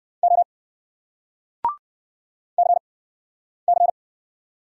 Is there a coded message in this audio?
Yes: Morse code from 0.2 to 3.9 s, heard as a keyed tone, about 60 wpm, spaced as for 9 wpm, 700 Hz. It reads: O CC